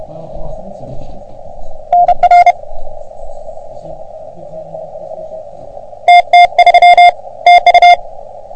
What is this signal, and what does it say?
Morse code, heard as a keyed tone, about 33 wpm, 720 Hz. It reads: NR TT3 TU
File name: OM3PA PILUP CUC 388